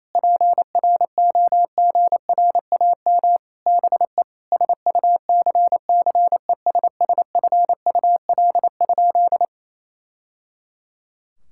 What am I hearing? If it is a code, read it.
Morse code, heard as a keyed tone, about 28 wpm, 700 Hz. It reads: PROGRAM BE SUCCESSFUL?
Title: Sentences from Top 1000 Words - Review Part 5 - 28wpm